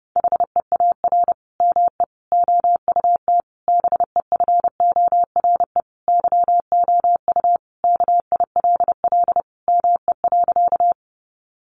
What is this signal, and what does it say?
Morse code, heard as a keyed tone, about 30 wpm, 700 Hz. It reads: HEAR ME OUT BEFORE YOU KILL ME.
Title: Sentences from Top 1400 Words 30wpm